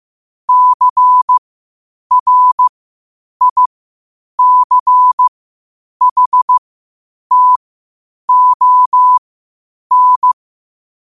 Morse code, heard as a keyed tone, about 15 wpm, spaced as for 8 wpm, 1000 Hz. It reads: CRICHTON
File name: qotdfiles / 8wpm